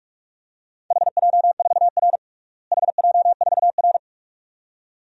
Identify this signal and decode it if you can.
Morse code, heard as a keyed tone, about 46 wpm, 700 Hz. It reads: HJ4R HJ4R
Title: DXCC-100 Call Signs - 4 Characters - Repeated 2x - 45wpm